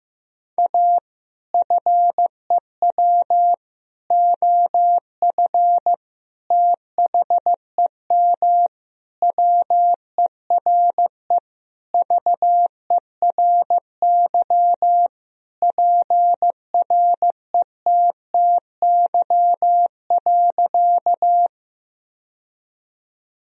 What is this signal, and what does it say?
Morse code, heard as a keyed tone, about 15 wpm, 700 Hz. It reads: A FEW OF THEM WERE VERY PRETTY.
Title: Sentences from Top 1000 Words - Review Part 1 - 15wpm